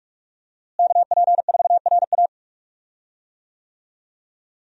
Morse code, heard as a keyed tone, about 46 wpm, 700 Hz. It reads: KP4RA
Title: DXCC-100 Call Signs - 5 Characters - Repeated 3x - 45wpm